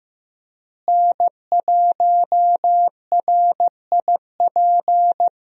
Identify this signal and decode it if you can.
Morse code, heard as a keyed tone, about 15 wpm, 700 Hz. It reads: N1RIP
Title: USA Call Signs - 5 Characters - Repeated 3x - 15wpm